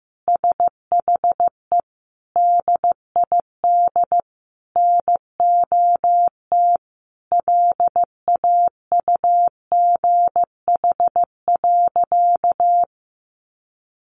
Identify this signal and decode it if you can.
Morse code, heard as a keyed tone, about 15 wpm, 700 Hz. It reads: SHE DID NOT LAUGH.
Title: Sentences from Top 1100 Words 15wpm